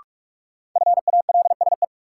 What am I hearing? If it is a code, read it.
Morse code, heard as a keyed tone, about 46 wpm, 700 Hz. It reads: FALSE
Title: Sentences from Top 1600 Words - New Word Drill Alt - 45wpm